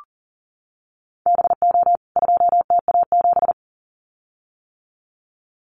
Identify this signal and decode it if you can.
Morse code, heard as a keyed tone, about 40 wpm, 700 Hz. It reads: BO 2TA7
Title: CWT Exchange 40wpm